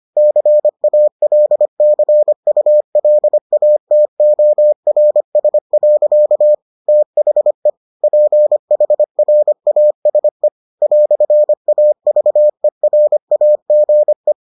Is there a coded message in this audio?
Yes: Morse code from 0.2 to 14.3 s, heard as a keyed tone, about 25 wpm, 600 Hz. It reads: CALCULATORS. THE PHRASE "AVERAGE